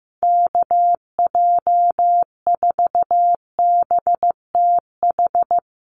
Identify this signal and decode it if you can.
Morse code, heard as a keyed tone, about 15 wpm, 700 Hz. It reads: KJ4BTH
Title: USA Call Signs Repeated 3x 15wpm